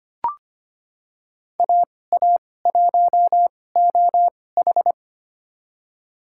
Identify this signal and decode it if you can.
Morse code, heard as a keyed tone, about 25 wpm, spaced as for 18 wpm, 700 Hz. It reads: AA1OH
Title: USA Call Signs - 5 Characters - 2x Character Spacing - 25wpm